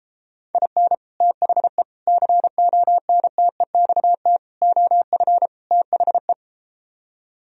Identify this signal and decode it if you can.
Morse code, heard as a keyed tone, about 33 wpm, 700 Hz. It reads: IN THE CONTEXT OF THE